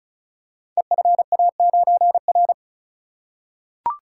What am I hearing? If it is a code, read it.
Morse code, heard as a keyed tone, about 35 wpm, 700 Hz. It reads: EFA9R
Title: Code Groups - 5 Letter:Numbers - 35wpm